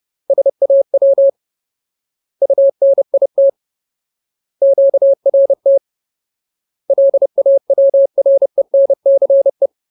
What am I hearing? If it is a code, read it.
Morse code, heard as a keyed tone, about 30 wpm, 550 Hz. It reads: SAW UNIT QRT LAWRENCE
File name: Ultimate Word List / 30WPM Ultimate